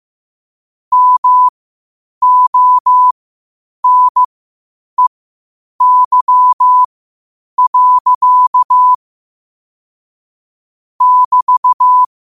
Morse code, heard as a keyed tone, about 15 wpm, spaced as for 8 wpm, 1000 Hz. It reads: MONEY. <BT>
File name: qotdfiles / 8wpm